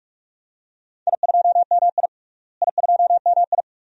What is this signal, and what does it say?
Morse code, heard as a keyed tone, about 46 wpm, 700 Hz. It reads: I2MI I2MI
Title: DXCC-100 Call Signs - 4 Characters - Repeated 2x - 45wpm